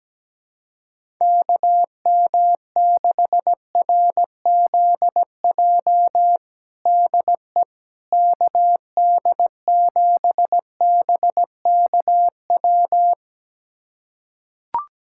Morse code, heard as a keyed tone, about 17 wpm, 700 Hz. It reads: KM6RZJ DE KD7BKW
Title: USA Callsign DE Callsign - 6 Characters - 17wpm